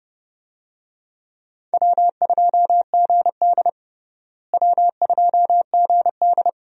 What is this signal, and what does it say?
Morse code, heard as a keyed tone, about 30 wpm, 700 Hz. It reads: W2GD W2GD
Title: USA Call Signs - 4 Characters - Repeated 2x - 30wpm